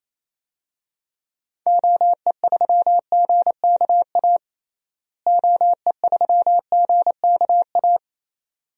Morse code, heard as a keyed tone, about 28 wpm, 700 Hz. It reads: OE3GKA OE3GKA
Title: DXCC-100 Call Signs - 6 Characters - Repeated 2x - 28wpm